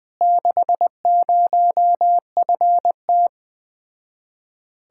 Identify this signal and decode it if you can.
Morse code, heard as a keyed tone, about 20 wpm, 700 Hz. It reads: 60FT
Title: Speed-Racing Common QSO Phrases - 17wpm